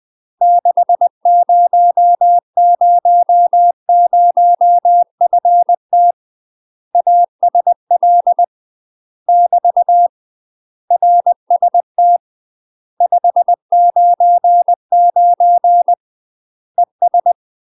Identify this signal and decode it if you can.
Morse code, heard as a keyed tone, about 20 wpm, 700 Hz. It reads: 6000FT ASL <BT> RST 599 ES